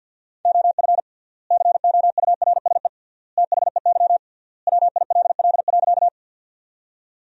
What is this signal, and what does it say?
Morse code, heard as a keyed tone, about 50 wpm, 700 Hz. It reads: OF COURSE THEY WILL.
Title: 3x Word-Spacing - Sentences from Top 500 Words - 50wpm